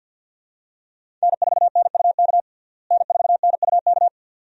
Morse code, heard as a keyed tone, about 52 wpm, 700 Hz. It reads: N4NUK N4NUK
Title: USA Call Signs - 5 Characters - Repeated 2x - 50wpm